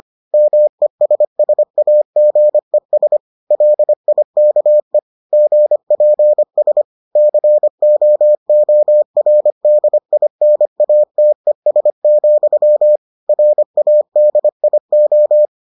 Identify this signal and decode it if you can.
Morse code, heard as a keyed tone, about 25 wpm, 600 Hz. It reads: MESSAGES LIKE GPS COORDINATES, RADIO